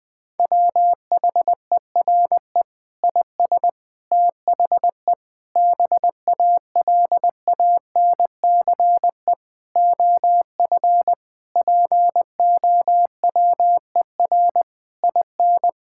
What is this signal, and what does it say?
Morse code, heard as a keyed tone, about 21 wpm, 700 Hz. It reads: WHERE IS THE BALANCE OF POWER IN
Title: Sentences from Top 1000 Words - Review Part 6 - 20wpm